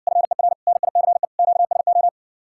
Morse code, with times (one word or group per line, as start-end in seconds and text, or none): 0.1-0.5 FER
0.7-1.2 NICE
1.4-2.1 QSO